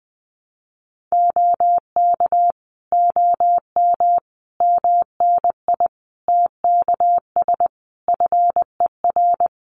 Morse code, heard as a keyed tone, about 20 wpm, 700 Hz. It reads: OK OM MNI TKS FER
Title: Common QSO Phrases 20wpm